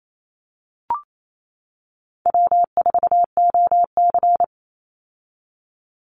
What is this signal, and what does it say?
Morse code, heard as a keyed tone, about 28 wpm, 700 Hz. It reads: W4OC
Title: Easy Call Signs - 28wpm